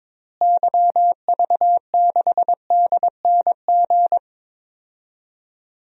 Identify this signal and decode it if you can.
Morse code, heard as a keyed tone, about 22 wpm, 700 Hz. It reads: YV6DNG
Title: DXCC-100 Call Signs - 6 Characters - Repeated 2x - 22wpm